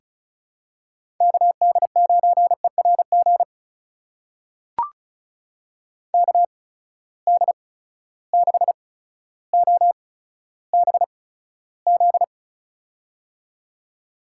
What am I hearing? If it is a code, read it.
Morse code, heard as a keyed tone, about 34 wpm, spaced as for 15 wpm, 700 Hz. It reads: KD9ERG KD6OBZ
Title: USA Call Signs - 6 Characters - 8x Character Spacing - 35wpm